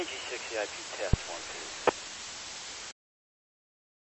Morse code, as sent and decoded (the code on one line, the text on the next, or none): none